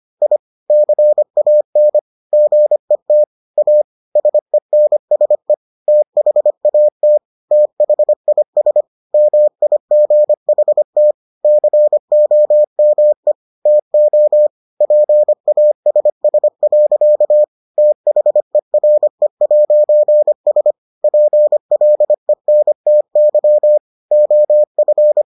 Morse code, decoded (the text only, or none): I CAN GET A SENSE THAT THIS MIGHT COME TO PASS. THERE'S PLENTY OF